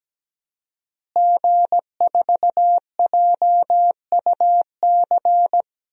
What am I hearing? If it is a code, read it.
Morse code, heard as a keyed tone, about 17 wpm, 700 Hz. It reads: G4JUC
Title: DXCC-100 Call Signs - 5 Characters - Repeated 3x - 17wpm